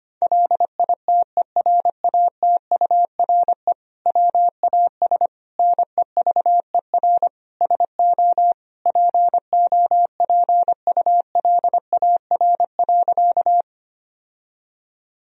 Morse code, heard as a keyed tone, about 25 wpm, 700 Hz. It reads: LITERATURE WAS NEVER SO POPULAR.
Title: Sentences from Top 1000 Words - Review Part 3 - 25wpm